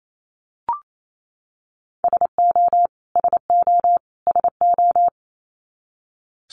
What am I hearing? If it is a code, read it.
Morse code, heard as a keyed tone, about 28 wpm, 700 Hz. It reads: SO SO SO
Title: Top 300 Words Repeated 28wpm